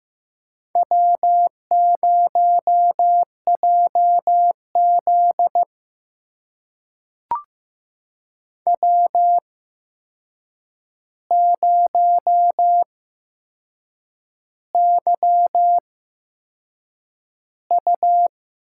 Morse code, heard as a keyed tone, about 15 wpm, spaced as for 6 wpm, 700 Hz. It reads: W0JZ W0YU
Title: USA Call Signs - 4 Characters - 8x Character Spacing - 15wpm